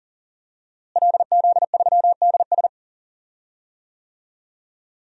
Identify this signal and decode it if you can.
Morse code, heard as a keyed tone, about 40 wpm, 700 Hz. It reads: LZ3DS
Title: DXCC-100 Call Signs - 5 Characters - Repeated 2x - 40wpm